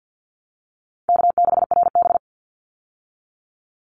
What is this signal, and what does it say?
Morse code, heard as a keyed tone, about 50 wpm, 700 Hz. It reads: K6RB
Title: Easy Call Signs - 50wpm